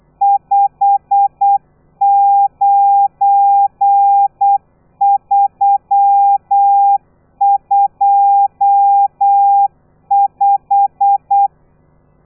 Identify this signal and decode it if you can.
Morse code, heard as a keyed tone, about 8 wpm, 800 Hz. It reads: 59325
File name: telegrafia / i2tfj / Lez12 35